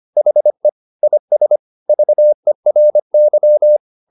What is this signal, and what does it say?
Morse code, heard as a keyed tone, about 25 wpm, 600 Hz. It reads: HE IS VERY
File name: the old man and the sea part 1 25wpm